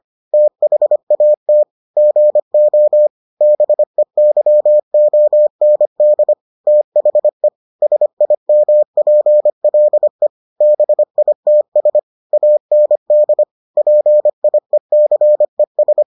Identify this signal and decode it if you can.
Morse code, heard as a keyed tone, about 25 wpm, 600 Hz. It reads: THAT GO BEYOND THE SIMPLE BITS AND PIECES